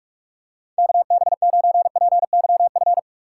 Morse code, heard as a keyed tone, about 46 wpm, 700 Hz. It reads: KB9PYF
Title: USA Call Signs - 6 Characters - 2x Character Spacing - 45wpm